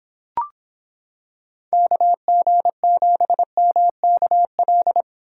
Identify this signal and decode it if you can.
Morse code, heard as a keyed tone, about 26 wpm, 700 Hz. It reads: KG7MKL